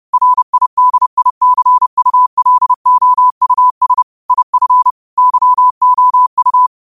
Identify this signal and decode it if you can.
Morse code, heard as a keyed tone, about 30 wpm, 1000 Hz. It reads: RIDICULOUS IF YOU